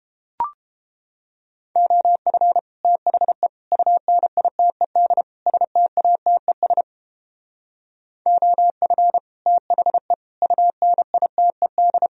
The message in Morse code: OF THE UNITED STATES OF THE UNITED

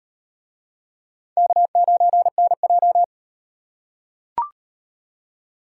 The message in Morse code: K9NJ

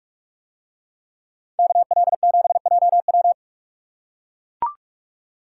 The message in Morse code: KL7JW

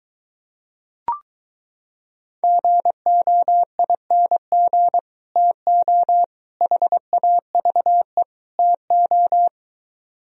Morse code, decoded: GOING TO HAVE TO

